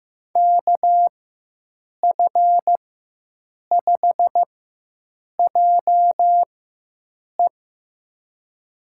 Morse code, heard as a keyed tone, about 16 wpm, spaced as for 7 wpm, 700 Hz. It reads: KF5JE
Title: USA Call Signs - 5 Characters - 4x Character Spacing - 15wpm